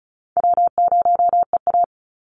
Morse code, heard as a keyed tone, about 34 wpm, 700 Hz. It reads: W0EA
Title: Easy Call Signs Repeated 3x - 35wpm